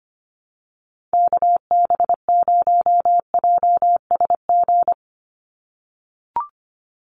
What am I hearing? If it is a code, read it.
Morse code, heard as a keyed tone, about 25 wpm, 700 Hz. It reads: KB0JSG
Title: USA Call Signs 25wpm